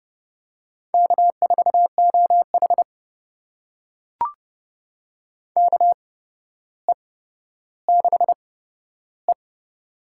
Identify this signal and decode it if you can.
Morse code, heard as a keyed tone, about 30 wpm, spaced as for 13 wpm, 700 Hz. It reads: K4OH KE6E
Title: USA Call Signs - 4 Characters - 8x Character Spacing - 30wpm